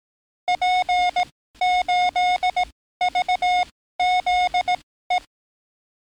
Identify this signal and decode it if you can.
Morse code, heard as a keyed tone, about 18 wpm, spaced as for 14 wpm, 720 Hz. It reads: P8VZE
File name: training / NAVPERS 11049to11060 / NAVPERS 11056 RA RB